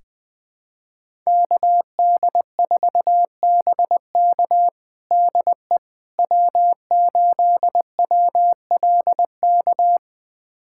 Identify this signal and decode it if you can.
Morse code, heard as a keyed tone, about 20 wpm, 700 Hz. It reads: KD4BK DE W8WLK
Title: USA Callsign DE Callsign 20wpm